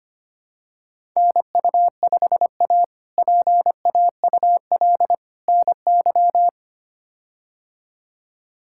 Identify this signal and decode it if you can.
Morse code, heard as a keyed tone, about 25 wpm, 700 Hz. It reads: NU5A PAUL NY
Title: K1USN SST QSO Practice - 25wpm